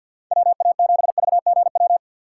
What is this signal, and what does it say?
Morse code, heard as a keyed tone, about 50 wpm, 700 Hz. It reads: WA7VGW
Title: USA Call Signs - 6 Characters - 4x Character Spacing - 50wpm